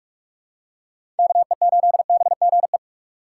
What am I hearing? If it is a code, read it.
Morse code, heard as a keyed tone, about 41 wpm, 690 Hz. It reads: KE8BGE
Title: USA Call Signs - 6 Characters - 2x Character Spacing - 45wpm